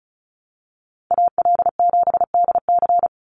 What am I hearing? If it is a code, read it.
Morse code, heard as a keyed tone, about 34 wpm, 700 Hz. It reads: AL7DC
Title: USA Call Signs 35wpm